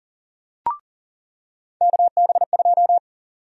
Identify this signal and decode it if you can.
Morse code, heard as a keyed tone, about 40 wpm, 700 Hz. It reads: KB2
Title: USA Call Sign Prefixes - 40wpm